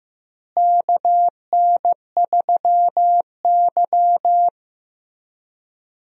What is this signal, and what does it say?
Morse code, heard as a keyed tone, about 15 wpm, 700 Hz. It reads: KN3Y
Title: USA Call Signs - 4 Characters - Repeated 3x - 15wpm